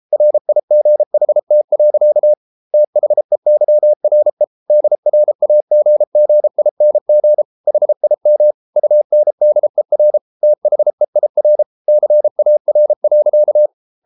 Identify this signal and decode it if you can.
Morse code, heard as a keyed tone, about 33 wpm, 600 Hz. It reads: RIGHT. THEYRE DRAGGING HIM UNDER THEIR CAR.